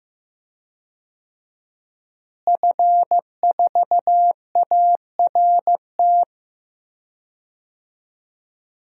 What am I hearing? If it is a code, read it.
Morse code, heard as a keyed tone, about 15 wpm, 700 Hz. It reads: F4ART
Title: DXCC-100 Call Signs - 5 Characters - Repeated 2x - 15wpm